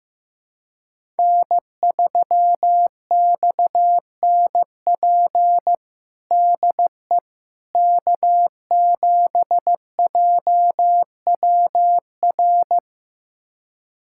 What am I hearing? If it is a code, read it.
Morse code, heard as a keyed tone, about 15 wpm, 700 Hz. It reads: N3XNP DE K7JWR
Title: USA Callsign DE Callsign - 5 Characters - 15wpm